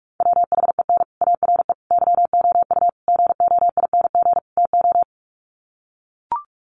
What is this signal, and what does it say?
Morse code, heard as a keyed tone, about 44 wpm, 700 Hz. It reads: WHEN ARE YOU GOING TO
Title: Sets of 5 Words Encore2 45wpm